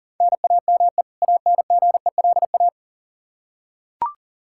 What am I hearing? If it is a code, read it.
Morse code, heard as a keyed tone, about 37 wpm, 700 Hz. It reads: NAME ANGELA
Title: Practice QSO - Exchange 1 - Beginning - 40wpm